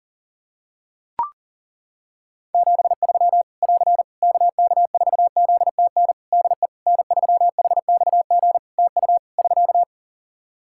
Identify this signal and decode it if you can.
Morse code, heard as a keyed tone, about 38 wpm, 700 Hz. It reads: 73 <AR> KK4ZTN DE N3HXG TU <SK>
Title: Practice QSO - Exchange 4 - Ending - 40wpm